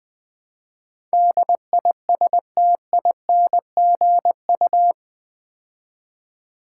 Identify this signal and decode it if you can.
Morse code, heard as a keyed tone, about 20 wpm, 700 Hz. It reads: DISTINGU
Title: Build A Word - 4 Letters At A Time - 11-Letter Words - 20wpm